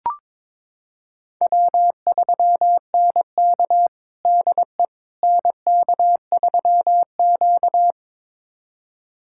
Morse code, heard as a keyed tone, about 22 wpm, 700 Hz. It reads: W3NK DE NK3Q
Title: USA Callsign DE Callsign - 4 Characters - 22wpm